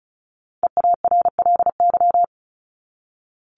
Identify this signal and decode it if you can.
Morse code, heard as a keyed tone, about 34 wpm, 700 Hz. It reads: EARLY
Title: Top 500 Words 35wpm